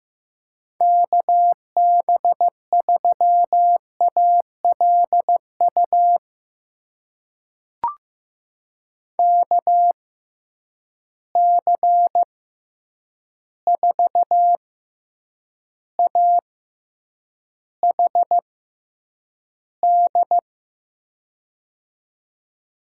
Morse code, heard as a keyed tone, about 15 wpm, spaced as for 8 wpm, 700 Hz. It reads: KB3ALU KC4AHD
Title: USA Call Signs - 6x Character Spacing - 15wpm